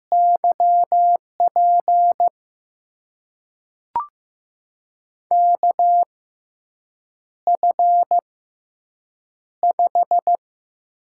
Morse code, heard as a keyed tone, about 15 wpm, spaced as for 7 wpm, 700 Hz. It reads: YP KF5
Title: USA Call Signs - 6 Characters - 6x Character Spacing - 15wpm